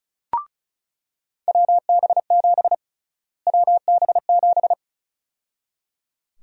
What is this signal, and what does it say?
Morse code, heard as a keyed tone, about 36 wpm, 700 Hz. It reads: WB7 WB7
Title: USA Call Sign Prefixes - Repeated 2x - 35wpm